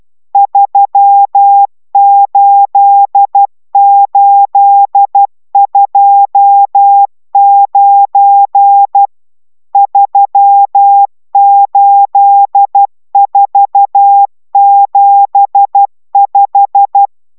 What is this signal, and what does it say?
Morse code, heard as a keyed tone, about 12 wpm, 800 Hz. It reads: 38829 38475